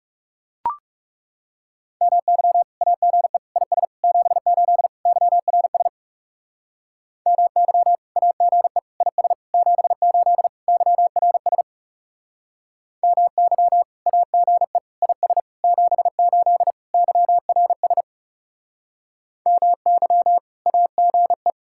Morse code, heard as a keyed tone, about 40 wpm, 700 Hz. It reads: MY AGE IS 78 YRS MY AGE IS 78 YRS MY AGE IS 78 YRS MY AGE